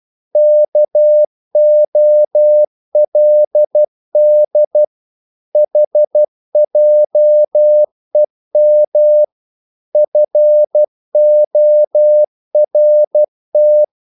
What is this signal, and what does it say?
Morse code, heard as a keyed tone, about 12 wpm, 600 Hz. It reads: KOLD HJEM FORT
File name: Korte ord DK 12wpm